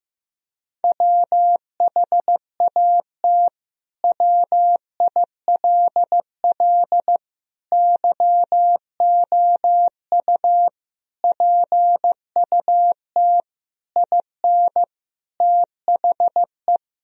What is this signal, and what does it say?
Morse code, heard as a keyed tone, about 15 wpm, 700 Hz. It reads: WHAT WILL YOU PUT IN THE
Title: Sentences from Top 1000 Words - Review Part 2 - 15wpm